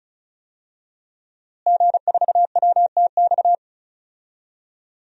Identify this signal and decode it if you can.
Morse code, heard as a keyed tone, about 36 wpm, 700 Hz. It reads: G4WTX
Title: DXCC-100 Call Signs - 5 Characters - Repeated 3x - 35wpm